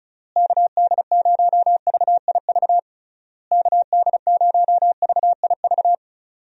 Morse code, heard as a keyed tone, about 34 wpm, 700 Hz. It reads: KD0VIV KD0VIV
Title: USA Call Signs - 6 Characters - Repeated 3x - 35wpm